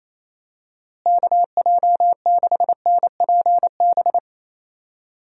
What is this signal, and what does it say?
Morse code, heard as a keyed tone, about 28 wpm, 700 Hz. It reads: KJ6NPB
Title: USA Call Signs - 6 Characters - Repeated 3x - 28wpm